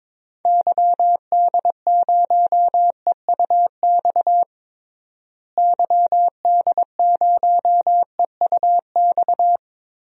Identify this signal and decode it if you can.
Morse code, heard as a keyed tone, about 22 wpm, 700 Hz. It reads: YD0EUX YD0EUX
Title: DXCC-100 Call Signs - 6 Characters - Repeated 3x - 22wpm